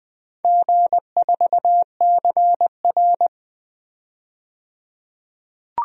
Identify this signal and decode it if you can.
Morse code, heard as a keyed tone, about 20 wpm, 700 Hz. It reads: G4CR
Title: DXCC-100 Call Signs - 4 Characters - 20wpm